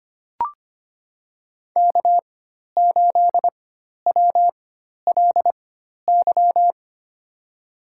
Morse code, heard as a keyed tone, about 25 wpm, spaced as for 12 wpm, 700 Hz. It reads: K8WLY
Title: USA Call Signs - 5 Characters - 4x Character Spacing - 25wpm